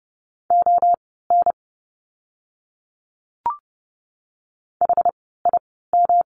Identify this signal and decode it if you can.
Morse code, heard as a keyed tone, about 30 wpm, spaced as for 17 wpm, 700 Hz. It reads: ON HIM
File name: course / 30:10wpm / 237 Lesson 38f - Letter q - Words 30wpm